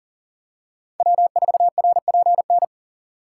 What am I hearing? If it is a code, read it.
Morse code, heard as a keyed tone, about 40 wpm, 700 Hz. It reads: W4RPN
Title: USA Call Signs - 5 Characters - 3x Character Spacing - 40wpm